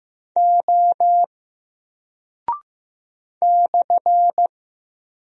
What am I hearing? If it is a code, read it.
Morse code, heard as a keyed tone, about 15 wpm, 700 Hz. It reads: O /
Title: Single Character (Letter:Number:Punctuation:Prosign) 15wpm